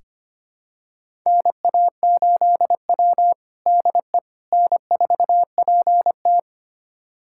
Morse code, heard as a keyed tone, about 25 wpm, 700 Hz. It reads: NA8W DE N4PT
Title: USA Callsign DE Callsign - 4 Characters - 25wpm